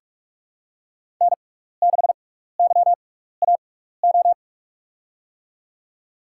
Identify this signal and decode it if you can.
Morse code, heard as a keyed tone, about 44 wpm, spaced as for 15 wpm, 700 Hz. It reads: N6YAO
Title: USA Call Signs - 5 Characters - 6x Character Spacing - 45wpm